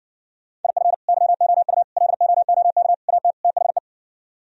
Morse code, heard as a keyed tone, about 58 wpm, 700 Hz. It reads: IF YOU LOOK AT THE